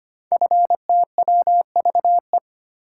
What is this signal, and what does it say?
Morse code, heard as a keyed tone, about 25 wpm, 700 Hz. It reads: FTWVE
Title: Code Groups - 5 Letters - 25wpm